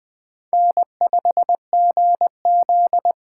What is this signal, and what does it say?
Morse code, heard as a keyed tone, about 20 wpm, 700 Hz. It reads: N5GZ
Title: Speed Racing - USA 1x2 Call Signs 17wpm